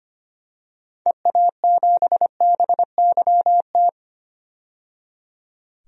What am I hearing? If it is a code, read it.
Morse code, heard as a keyed tone, about 25 wpm, 700 Hz. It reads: EA7BYT
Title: DXCC-100 Call Signs - 6 Characters - 25wpm